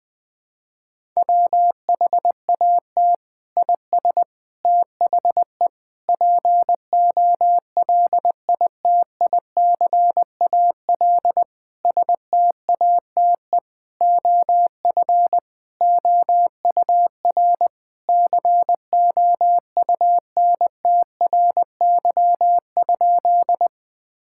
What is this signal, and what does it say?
Morse code, heard as a keyed tone, about 20 wpm, 700 Hz. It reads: WHAT IS THE POLITICAL STATE OF OUR COUNTRY?
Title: Sentences from Top 1000 Words - Review Part 5 - 20wpm